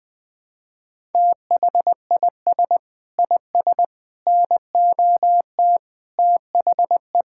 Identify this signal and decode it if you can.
Morse code, heard as a keyed tone, about 20 wpm, 700 Hz. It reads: THIS IS NOT THE
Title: Sentences with 7-Word Phrases - 20wpm